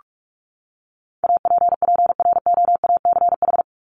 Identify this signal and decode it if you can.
Morse code, heard as a keyed tone, about 44 wpm, 700 Hz. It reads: APPROACH
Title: Top 751-1000 Words 45wpm